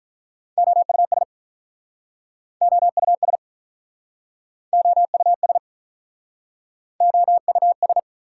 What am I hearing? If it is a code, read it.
Morse code, heard as a keyed tone, about 48 wpm, 700 Hz. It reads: OUS OUS OUS OUS